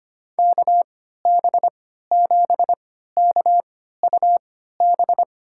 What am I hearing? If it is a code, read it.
Morse code, heard as a keyed tone, about 25 wpm, spaced as for 14 wpm, 700 Hz. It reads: KB7KUB
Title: USA Call Signs - 6 Characters - 3x Character Spacing - 25wpm